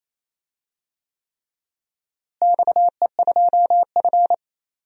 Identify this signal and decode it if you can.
Morse code, heard as a keyed tone, about 27 wpm, 700 Hz. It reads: XE2F
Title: DXCC-100 Call Signs - 4 Characters - Repeated 3x - 28wpm